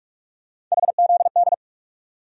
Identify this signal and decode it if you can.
Morse code, heard as a keyed tone, about 46 wpm, 700 Hz. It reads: HZD
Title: USA Call Sign Postfixes - 3 Letters 45wpm